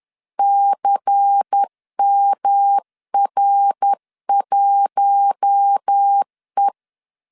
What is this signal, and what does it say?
Morse code, heard as a keyed tone, about 10 wpm, 800 Hz. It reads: CMR1E
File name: LEZIONI / 8 YJ45 / 8 YJ4 10wpm 2633